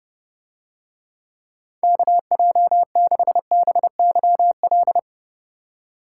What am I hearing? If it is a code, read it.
Morse code, heard as a keyed tone, about 29 wpm, 700 Hz. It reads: KJ6BYL